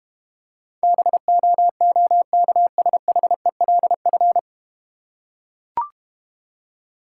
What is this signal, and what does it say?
Morse code, heard as a keyed tone, about 31 wpm, 700 Hz. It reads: BOOKSHELF
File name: practice / 25wpm / Difficult Words - Speed-Racing with Hints - 25wpm